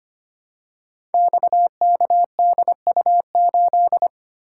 Morse code, heard as a keyed tone, about 25 wpm, 700 Hz. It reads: XKDU8
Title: Code Groups - 5 Letter:Numbers - 25wpm